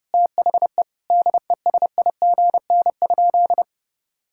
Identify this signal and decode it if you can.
Morse code, heard as a keyed tone, about 32 wpm, 700 Hz. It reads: THE DESIGN?
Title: Sentences from Top 1000 Words - Review Part 5 - 30wpm